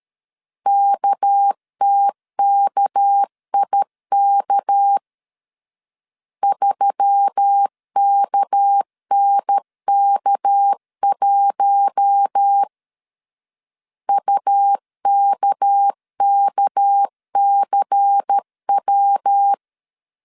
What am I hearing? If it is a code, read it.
Morse code, heard as a keyed tone, about 13 wpm, 800 Hz. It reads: KTKIK 3KNK1 UKKCW
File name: LEZIONI / 6 PKX0 / 6 PK 12wpm 2006